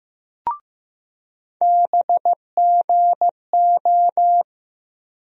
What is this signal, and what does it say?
Morse code, heard as a keyed tone, about 15 wpm, 700 Hz. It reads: BGO
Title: USA Call Sign Postfixes - 3 Letters 15wpm